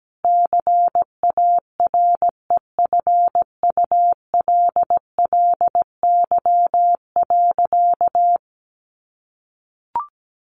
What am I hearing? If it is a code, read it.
Morse code, heard as a keyed tone, about 17 wpm, 700 Hz. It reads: CAREFULLY.
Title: Sentences from Top 1500 Words 17wpm